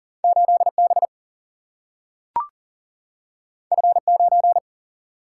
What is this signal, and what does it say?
Morse code, heard as a keyed tone, about 40 wpm, 700 Hz. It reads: 8B F9
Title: Single-Letter Single-Number 40wpm